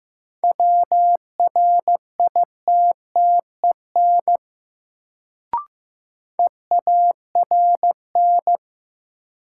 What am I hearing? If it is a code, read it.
Morse code, heard as a keyed tone, about 15 wpm, 700 Hz. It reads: WRITTEN EARN
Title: Sentences from Top 1500 Words - New Word Drill Alt - 15wpm